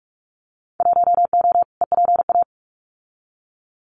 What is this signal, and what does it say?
Morse code, heard as a keyed tone, about 46 wpm, 700 Hz. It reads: 1O EPA
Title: Winter Field Day Exchanges 45wpm